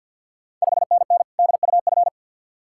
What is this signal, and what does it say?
Morse code, heard as a keyed tone, about 50 wpm, 690 Hz. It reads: 5NN DUF
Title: Ontario QSO Party Exchange 50wpm